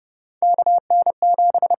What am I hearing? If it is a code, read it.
Morse code, heard as a keyed tone, about 30 wpm, 700 Hz. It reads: KN7
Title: USA Call Signs - 4 Characters - 8x Character Spacing - 30wpm